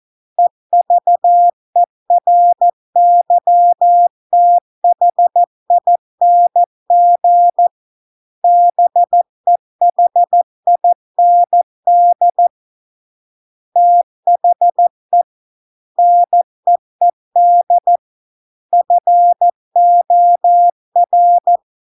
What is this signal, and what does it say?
Morse code, heard as a keyed tone, about 14 wpm, 700 Hz. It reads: EVERYTHING BEHIND THE NEED FOR